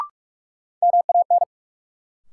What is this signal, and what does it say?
Morse code, heard as a keyed tone, about 46 wpm, 700 Hz. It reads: MAN
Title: Sets of 2 Words - New Word Drill - 45wpm